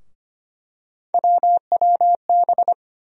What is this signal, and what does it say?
Morse code, heard as a keyed tone, about 25 wpm, 700 Hz. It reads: WWB
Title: USA Call Sign Postfixes - 3 Letters 25wpm